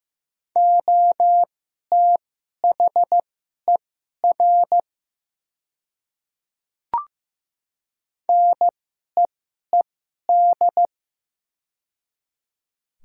Morse code, heard as a keyed tone, about 15 wpm, spaced as for 11 wpm, 700 Hz. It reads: OTHER NEED
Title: 2x Character Spacing - Top 200 Words - 15wpm